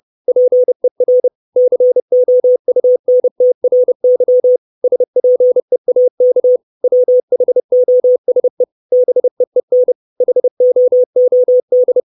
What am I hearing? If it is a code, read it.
Morse code, heard as a keyed tone, about 30 wpm, 500 Hz. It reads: PER COUNTRY SPEAK WHOSE BEEN HOOD
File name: 632 Words 30WPM